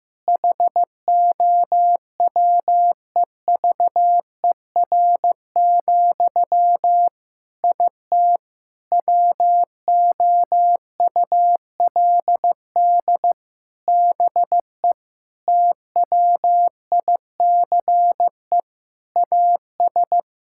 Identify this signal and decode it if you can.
Morse code, heard as a keyed tone, about 15 wpm, 700 Hz. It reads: HOWEVER, IT WOULD BE TWICE AS
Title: Sentences from Top 1600 Words 15wpm